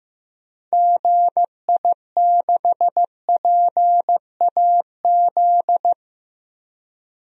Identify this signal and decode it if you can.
Morse code, heard as a keyed tone, about 15 wpm, 700 Hz. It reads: GI6PAZ
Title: DXCC-100 Call Signs - 6 Characters - Repeated 3x - 15wpm